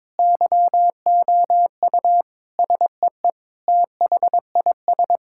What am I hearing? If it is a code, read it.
Morse code, heard as a keyed tone, about 21 wpm, 700 Hz. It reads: YOU SEE THIS
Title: Sentences from Top 1600 Words 22wpm